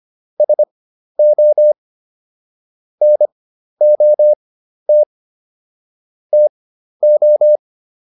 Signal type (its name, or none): Morse code